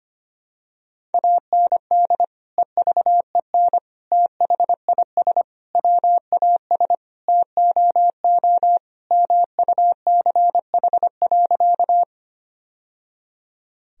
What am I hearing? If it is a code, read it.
Morse code, heard as a keyed tone, about 25 wpm, 700 Hz. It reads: AND EVEN THIS WAS TOO MUCH.
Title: Sentences from Top 1000 Words - Review Part 1 - 25wpm